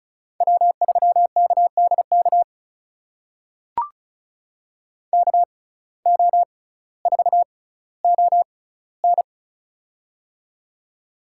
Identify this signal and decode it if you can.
Morse code, heard as a keyed tone, about 36 wpm, spaced as for 18 wpm, 700 Hz. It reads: W3KDK KO4ON
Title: USA Call Signs - 5 Characters - 6x Character Spacing - 35wpm